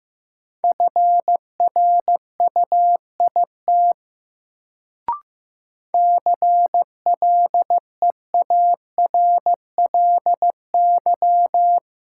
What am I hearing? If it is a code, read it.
Morse code, heard as a keyed tone, about 15 wpm, 700 Hz. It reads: FRUIT CLEARLY
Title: Sentences from Top 1300 Words - New Word Drill - 15wpm